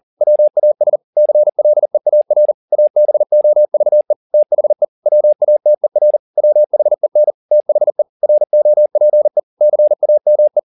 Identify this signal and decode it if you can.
Morse code, heard as a keyed tone, about 40 wpm, 600 Hz. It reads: WAS CLEAR ABOVE THE WATER WHEN THE ROPE CAME
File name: the old man and the sea 2 40wpm